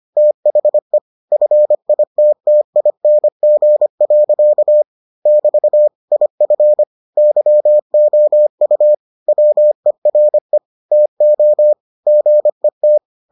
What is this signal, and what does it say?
Morse code, heard as a keyed tone, about 25 wpm, 600 Hz. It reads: THE FITTING. <BT> IF YOU WERE TO GET